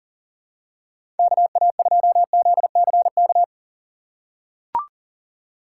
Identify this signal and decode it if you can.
Morse code, heard as a keyed tone, about 40 wpm, 700 Hz. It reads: KA2ZCK
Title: USA Call Signs - 3x Character Spacing - 40wpm